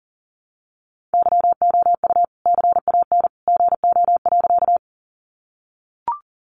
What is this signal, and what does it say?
Morse code, heard as a keyed tone, about 40 wpm, 700 Hz. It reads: YOU CAN GO.
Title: Sentences from Top 100 Words 40wpm